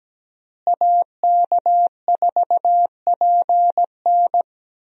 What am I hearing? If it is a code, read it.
Morse code, heard as a keyed tone, about 17 wpm, 700 Hz. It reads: AK4PN
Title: USA Call Signs - 5 Characters - Repeated 2x - 17wpm